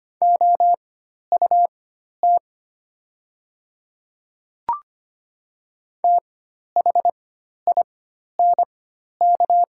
Morse code, heard as a keyed tone, about 25 wpm, spaced as for 12 wpm, 700 Hz. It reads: OUT THINK
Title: 4x Character Spacing - Top 200 Words - 25wpm